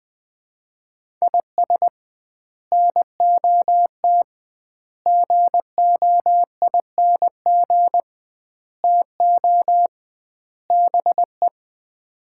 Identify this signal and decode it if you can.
Morse code, heard as a keyed tone, about 21 wpm, 700 Hz. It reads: IS NOT GOING TO BE
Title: 2x Word-Spacing - Sets of 5 Words - 20wpm